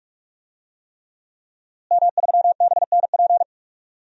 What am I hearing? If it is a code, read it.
Morse code, heard as a keyed tone, about 46 wpm, 700 Hz. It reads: M3BNP
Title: DXCC-100 Call Signs - 5 Characters - Repeated 3x - 45wpm